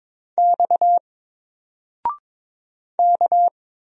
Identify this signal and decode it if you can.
Morse code, heard as a keyed tone, about 22 wpm, 700 Hz. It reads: X K
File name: Single-Letter-Number-22wpm